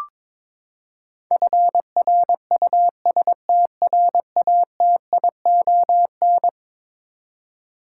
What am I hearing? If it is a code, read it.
Morse code, heard as a keyed tone, about 22 wpm, 700 Hz. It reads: FRUSTRATION
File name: practice / 22wpm / Eleven Letter Words - 22wpm